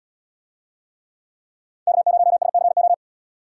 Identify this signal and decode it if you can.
Morse code, heard as a keyed tone, about 77 wpm, 700 Hz. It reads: K0SCO